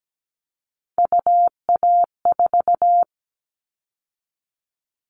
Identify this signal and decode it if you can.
Morse code, heard as a keyed tone, about 17 wpm, 700 Hz. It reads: UA4
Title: Callsign Prefixes for Top 50 DXCC - 17wpm